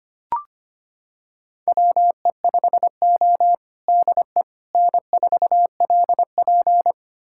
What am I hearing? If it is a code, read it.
Morse code, heard as a keyed tone, about 25 wpm, 700 Hz. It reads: WE5O DE N4LP
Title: USA Callsign DE Callsign - 4 Characters - 25wpm